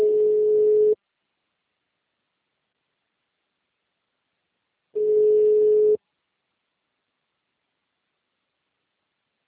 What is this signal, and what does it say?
Ringback tone (Europe, 425 Hz).
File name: out- -2007- - - .53689